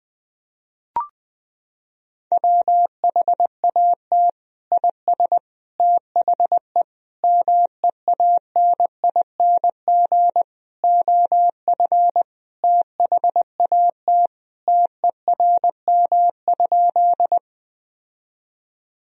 Morse code, heard as a keyed tone, about 21 wpm, 700 Hz. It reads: WHAT IS THE MEANING OF THAT TERM?
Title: Sentences from Top 1000 Words - Review Part 4 - 20wpm